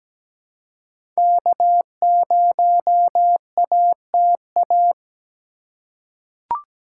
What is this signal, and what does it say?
Morse code, heard as a keyed tone, about 17 wpm, 700 Hz. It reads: K0ATA
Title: USA Call Signs - 5 Characters - 3x Character Spacing - 17wpm